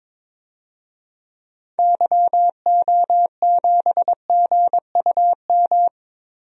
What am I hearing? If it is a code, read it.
Morse code, heard as a keyed tone, about 22 wpm, 700 Hz. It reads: YO7GUM
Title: DXCC-100 Call Signs - 6 Characters - Repeated 2x - 22wpm